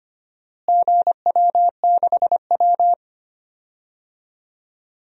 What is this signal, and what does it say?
Morse code, heard as a keyed tone, about 25 wpm, 700 Hz. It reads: GW6W
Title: DXCC-100 Call Signs - 4 Characters - Repeated 2x - 25wpm